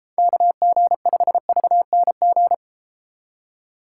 Morse code, heard as a keyed tone, about 32 wpm, 700 Hz. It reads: KG5VNG